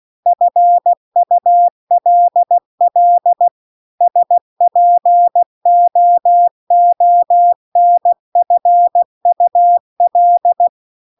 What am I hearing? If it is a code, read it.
Morse code, heard as a keyed tone, about 16 wpm, 700 Hz. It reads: FULL SPOONFUL